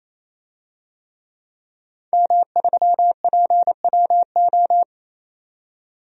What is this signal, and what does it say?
Morse code, heard as a keyed tone, about 28 wpm, 700 Hz. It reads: M3PWO
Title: DXCC-100 Call Signs - 5 Characters - 28wpm